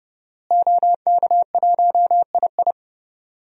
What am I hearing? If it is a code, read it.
Morse code, heard as a keyed tone, about 30 wpm, 700 Hz. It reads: OK1II
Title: DXCC-100 Call Signs - 5 Characters - Repeated 3x - 30wpm